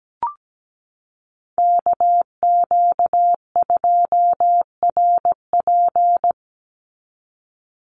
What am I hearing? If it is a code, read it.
Morse code, heard as a keyed tone, about 17 wpm, 700 Hz. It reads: KQ2RP
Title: NA SOTA Chaser Callsigns 17wpm